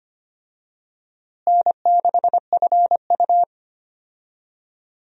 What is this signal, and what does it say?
Morse code, heard as a keyed tone, about 25 wpm, 700 Hz. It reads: N6FU
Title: USA Call Signs - 4 Characters - Repeated 3x - 25wpm